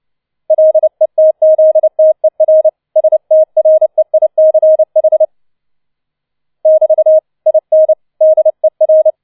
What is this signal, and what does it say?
Morse code, heard as a keyed tone, about 29 wpm, 630 Hz. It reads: LETZTER STREICH <BT> IN DER
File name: 006 Sechster Streich 30wpm